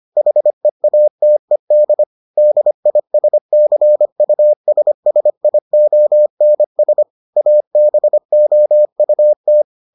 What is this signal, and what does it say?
Morse code, heard as a keyed tone, about 25 wpm, 600 Hz. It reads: HEATED DISCUSSIONS ABOUT